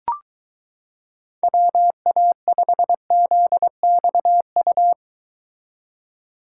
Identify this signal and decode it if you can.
Morse code, heard as a keyed tone, about 23 wpm, 700 Hz. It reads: WA5ZXU